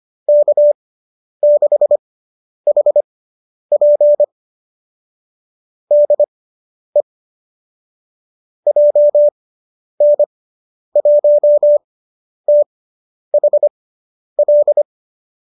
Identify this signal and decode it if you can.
Morse code, heard as a keyed tone, about 25 wpm, spaced as for 10 wpm, 590 Hz. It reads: K6HP DE JN1THL